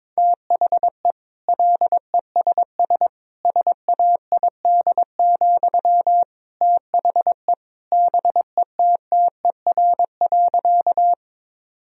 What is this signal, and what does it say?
Morse code, heard as a keyed tone, about 22 wpm, 700 Hz. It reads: THE LESS SAID, THE BETTER.
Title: Sentences from Top 1000 Words - Review Part 2 - 22wpm